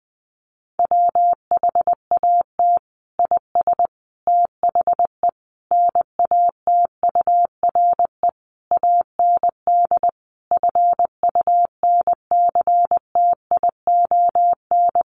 Morse code, heard as a keyed tone, about 20 wpm, 700 Hz. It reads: WHAT IS THE NATURE AND FUNCTION
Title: Sentences from Top 1100 Words 20wpm